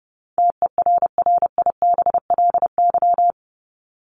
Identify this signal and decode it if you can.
Morse code, heard as a keyed tone, about 30 wpm, 700 Hz. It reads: TERRIBLY
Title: Top 1751-2000 Words 30wpm